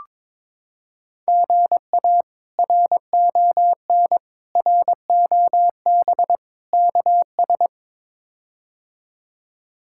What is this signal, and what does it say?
Morse code, heard as a keyed tone, about 22 wpm, 700 Hz. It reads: GA RON ROB KS
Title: K1USN SST QSO Practice - 22wpm